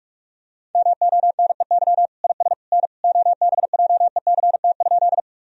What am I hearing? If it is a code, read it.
Morse code, heard as a keyed tone, about 45 wpm, 690 Hz. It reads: MONEY IS N OBJECT?
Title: Sentences from Top 500 Words and Abbreviations 45wpm